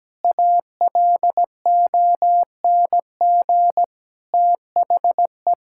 Morse code, heard as a keyed tone, about 17 wpm, 700 Hz. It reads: ALONG THE
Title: Sentences from Top 1000 Words - Review Part 6 - 17wpm